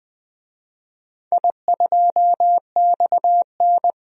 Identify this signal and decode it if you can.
Morse code, heard as a keyed tone, about 20 wpm, 700 Hz. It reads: I2XN
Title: DXCC-100 Call Signs - 4 Characters - 20wpm